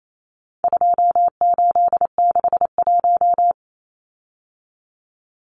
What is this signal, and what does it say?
Morse code, heard as a keyed tone, about 28 wpm, 700 Hz. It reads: 2861
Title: Numbers - 4-Digit 28wpm